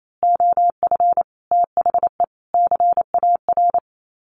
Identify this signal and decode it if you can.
Morse code, heard as a keyed tone, about 28 wpm, 700 Hz. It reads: OF THE CAR
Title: Sets of 5 Words Encore2 28wpm